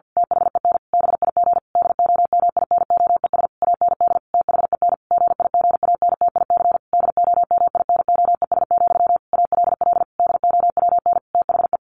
Morse code, heard as a keyed tone, about 50 wpm, 700 Hz. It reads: THEN BIG DOMINOES AND THEN GIGANTIC DOMINOES, ALL DOWN THE
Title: A Sound Of Thunder 50wpm